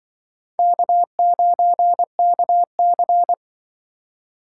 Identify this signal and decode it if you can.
Morse code, heard as a keyed tone, about 25 wpm, 700 Hz. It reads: K9KC